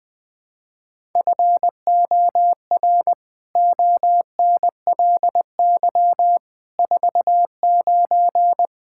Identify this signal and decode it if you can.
Morse code, heard as a keyed tone, about 20 wpm, 700 Hz. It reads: FOR ONLY 49
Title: Pangram Challenge v2 - 20wpm